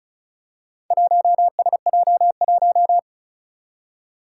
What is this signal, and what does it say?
Morse code, heard as a keyed tone, about 36 wpm, 700 Hz. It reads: 1SJ1
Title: Four Challenging Letter-Number Combinations 35wpm